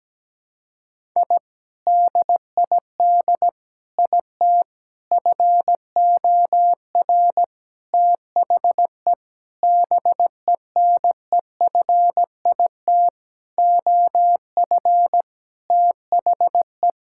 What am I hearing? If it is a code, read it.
Morse code, heard as a keyed tone, about 17 wpm, 700 Hz. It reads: I DID IT FOR THE BENEFIT OF THE
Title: Sentences from Top 1000 Words - Review Part 6 - 17wpm